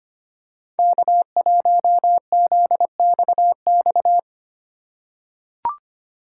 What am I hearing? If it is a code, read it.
Morse code, heard as a keyed tone, about 25 wpm, 700 Hz. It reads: K1ZXX
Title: USA Call Signs - 4x Character Spacing - 25wpm